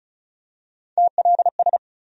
Morse code, heard as a keyed tone, about 34 wpm, 700 Hz. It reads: TLS
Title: USA Call Sign Postfixes - 3 Letters 35wpm